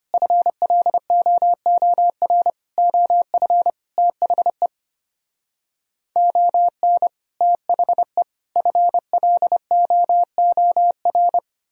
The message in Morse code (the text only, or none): FLOOR OF THE ON THE FLOOR